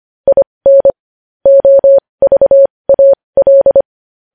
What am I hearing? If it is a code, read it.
Morse code, heard as a keyed tone, about 25 wpm, spaced as for 20 wpm, 550 Hz. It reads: IN OVAL
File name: podcast / News headlines-1x-25x20